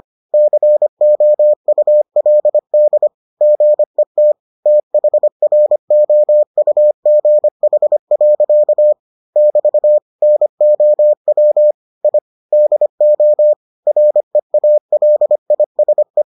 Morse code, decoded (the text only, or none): COULD GET THROUGH. <BT> NOW I DO REALISE